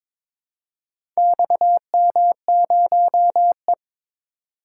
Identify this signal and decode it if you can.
Morse code, heard as a keyed tone, about 22 wpm, 700 Hz. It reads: XM0E
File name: practice / 22wpm / Four Letter-Number Combinations - 22wpm